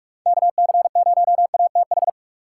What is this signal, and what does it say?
Morse code, heard as a keyed tone, about 44 wpm, 700 Hz. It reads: KC0ATH